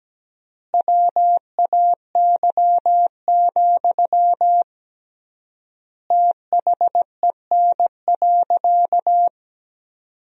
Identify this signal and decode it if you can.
Morse code, heard as a keyed tone, about 17 wpm, 700 Hz. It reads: WAY, THEN.
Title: 3x Word-Spacing - Sentences from Top 400 Words - 17wpm